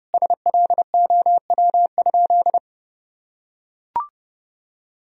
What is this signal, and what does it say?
Morse code, heard as a keyed tone, about 30 wpm, 700 Hz. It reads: SLOW?
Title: Sentences from Top 1000 Words - Review Part 2 - 30wpm